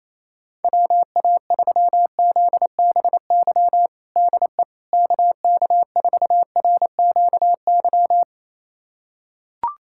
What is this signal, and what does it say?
Morse code, heard as a keyed tone, about 28 wpm, 700 Hz. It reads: WA3ZBY DE KK4RQY
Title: USA Callsign DE Callsign 28wpm